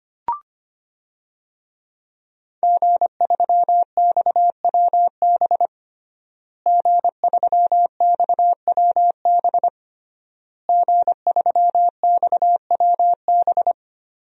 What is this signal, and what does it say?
Morse code, heard as a keyed tone, about 25 wpm, 700 Hz. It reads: G3XWB G3XWB G3XWB
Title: DXCC-100 Call Signs - 5 Characters - Repeated 3x - 25wpm